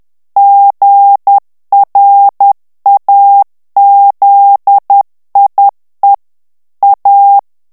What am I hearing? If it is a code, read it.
Morse code, heard as a keyed tone, about 10 wpm, 800 Hz. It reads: GRAZIE A
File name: morse mp3 / ITALIEN texte clair 1 vitesse 10 mots minute